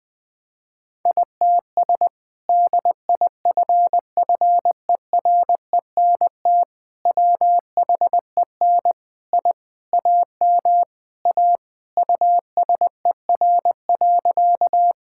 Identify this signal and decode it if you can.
Morse code, heard as a keyed tone, about 21 wpm, 700 Hz. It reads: ITS DIFFERENT WHEN I AM A USER.
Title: Sentences from Top 1000 Words - Review Part 5 - 20wpm